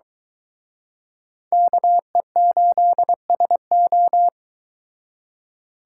Morse code, heard as a keyed tone, about 22 wpm, 700 Hz. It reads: KE8SO